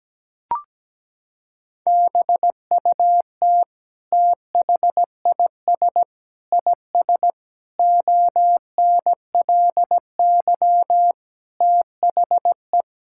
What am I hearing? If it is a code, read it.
Morse code, heard as a keyed tone, about 17 wpm, 700 Hz. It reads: BUT THIS IS ONLY THE
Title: Sentences from Top 1000 Words - Review Part 5 - 17wpm